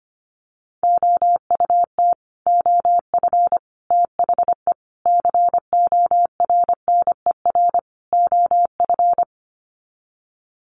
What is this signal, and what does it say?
Morse code, heard as a keyed tone, about 25 wpm, 700 Hz. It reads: OUT OF THE CORNER OF
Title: Sets of 5 Words 25wpm